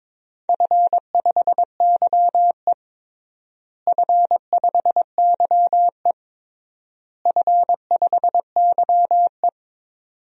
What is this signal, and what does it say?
Morse code, heard as a keyed tone, about 22 wpm, 700 Hz. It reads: F5YE F5YE F5YE
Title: DXCC-100 Call Signs - 4 Characters - Repeated 3x - 22wpm